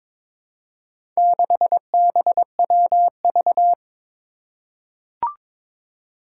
Morse code, heard as a keyed tone, about 23 wpm, 700 Hz. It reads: 6BWV
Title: Four Challenging Letter-Number Combinations 22wpm